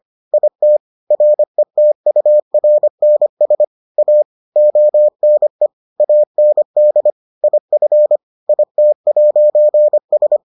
Morse code, heard as a keyed tone, about 25 wpm, 600 Hz. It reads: IT RETURNS A ONE AND IF IT'S